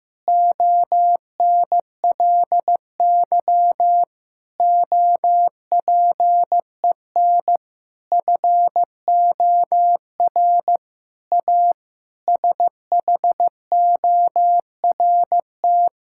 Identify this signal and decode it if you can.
Morse code, heard as a keyed tone, about 15 wpm, 700 Hz. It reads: ONLY OPEN FOR A SHORT
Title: Sentences from Top 1600 Words 15wpm